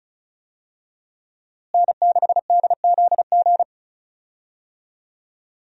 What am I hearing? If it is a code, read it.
Morse code, heard as a keyed tone, about 34 wpm, 700 Hz. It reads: N6DZG
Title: USA Call Signs - 5 Characters - Repeated 3x - 35wpm